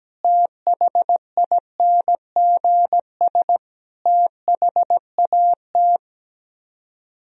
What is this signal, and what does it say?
Morse code, heard as a keyed tone, about 17 wpm, 700 Hz. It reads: THINGS THAT
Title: Speed-Racing - Sets of 5 Words 17wpm